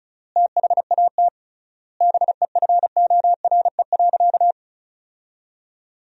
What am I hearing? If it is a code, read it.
Morse code, heard as a keyed tone, about 36 wpm, 700 Hz. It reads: THAT BEFORE.
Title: 3x Word-Spacing - Sentences from Top 400 Words - 35wpm